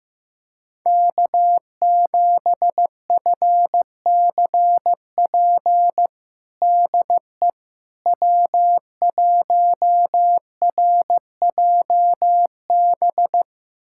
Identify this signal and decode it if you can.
Morse code, heard as a keyed tone, about 15 wpm, 700 Hz. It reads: K7FCP DE W1RJB
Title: USA Callsign DE Callsign - 5 Characters - 15wpm